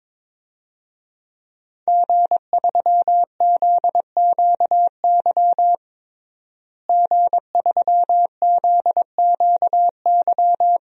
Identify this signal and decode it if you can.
Morse code, heard as a keyed tone, about 22 wpm, 700 Hz. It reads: G3ZQY G3ZQY
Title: DXCC-100 Call Signs - 5 Characters - Repeated 3x - 22wpm